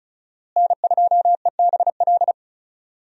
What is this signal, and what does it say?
Morse code, heard as a keyed tone, about 36 wpm, 700 Hz. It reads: N2EBL
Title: USA Call Signs - 5 Characters - Repeated 3x - 35wpm